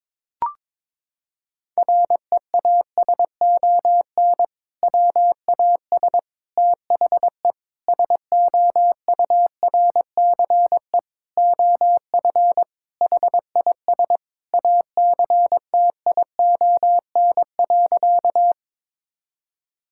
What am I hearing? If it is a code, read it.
Morse code, heard as a keyed tone, about 22 wpm, 700 Hz. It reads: REASON WAS THE SOURCE OF HIS ACTION.
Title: Sentences from Top 1000 Words - Review Part 5 - 22wpm